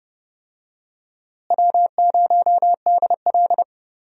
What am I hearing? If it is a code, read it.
Morse code, heard as a keyed tone, about 30 wpm, 700 Hz. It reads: W0DL
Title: USA Call Signs - 4 Characters - Repeated 2x - 30wpm